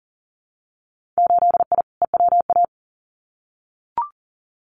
Morse code, heard as a keyed tone, about 40 wpm, 700 Hz. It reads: 8I EWA
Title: Winter Field Day Exchanges Encore 40wpm